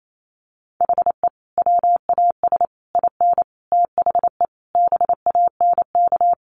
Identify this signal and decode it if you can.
Morse code, heard as a keyed tone, about 28 wpm, 700 Hz. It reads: HE WAS IN THE BANK
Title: Sentences from Top 900 Words 28wpm